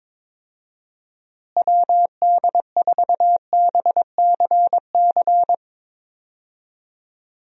Morse code, heard as a keyed tone, about 22 wpm, 700 Hz. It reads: WD4BCC
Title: USA Call Signs - 6 Characters - Repeated 3x - 22wpm